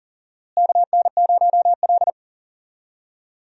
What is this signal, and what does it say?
Morse code, heard as a keyed tone, about 40 wpm, 700 Hz. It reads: KN0L
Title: USA Call Signs - 4 Characters - Repeated 2x - 40wpm